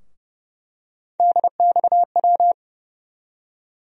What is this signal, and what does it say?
Morse code, heard as a keyed tone, about 30 wpm, 700 Hz. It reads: DXW
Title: USA Call Sign Postfixes - 3 Letters 30wpm